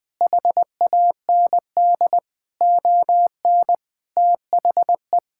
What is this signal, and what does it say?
Morse code, heard as a keyed tone, about 20 wpm, 700 Hz. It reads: HAND ON THE
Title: Sentences from Top 1000 Words - Review Part 6 - 20wpm